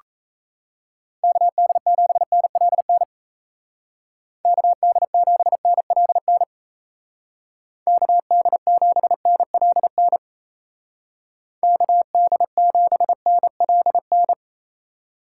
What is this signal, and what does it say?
Morse code, heard as a keyed tone, about 37 wpm, 700 Hz. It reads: KD7NLN KD7NLN KD7NLN KD7NLN